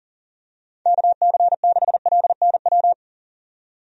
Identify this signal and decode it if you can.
Morse code, heard as a keyed tone, about 37 wpm, 700 Hz. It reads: KC6LNW
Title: USA Call Signs - 6 Characters - 40wpm